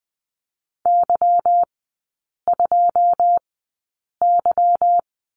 Morse code, heard as a keyed tone, about 20 wpm, spaced as for 8 wpm, 690 Hz. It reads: Y2Y
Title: 145 Lesson 24e - Letter y - Three Characters 20wpm